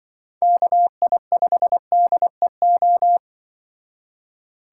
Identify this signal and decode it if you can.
Morse code, heard as a keyed tone, about 25 wpm, 700 Hz. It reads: KI5DEO